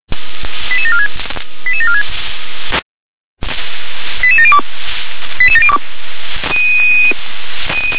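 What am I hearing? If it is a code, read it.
Five-tone selective call (ZVEI), heard as quick runs of five tones, tones of about 70 ms. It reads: 88746 88746 88702 88702
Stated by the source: Received on SDR.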